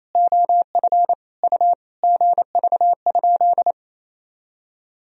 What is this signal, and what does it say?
Morse code, heard as a keyed tone, about 28 wpm, 700 Hz. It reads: OF U GV?
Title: Sentences from Top 200 Words and Abbreviations 28wpm